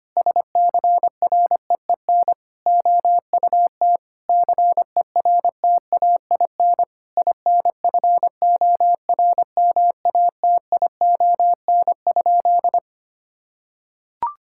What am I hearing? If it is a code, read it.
Morse code, heard as a keyed tone, about 25 wpm, 700 Hz. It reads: SCREEN OUT CERTAIN INFORMATION?
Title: Sentences from Top 1000 Words - Review Part 6 - 25wpm